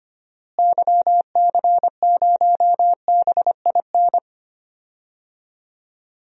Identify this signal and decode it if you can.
Morse code, heard as a keyed tone, about 25 wpm, 700 Hz. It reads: YC0BIN
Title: DXCC-100 Call Signs - 6 Characters - Repeated 3x - 25wpm